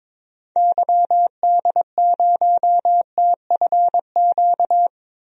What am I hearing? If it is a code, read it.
Morse code, heard as a keyed tone, about 22 wpm, 700 Hz. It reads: YD0TFQ
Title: DXCC-100 Call Signs - 6 Characters - Repeated 3x - 22wpm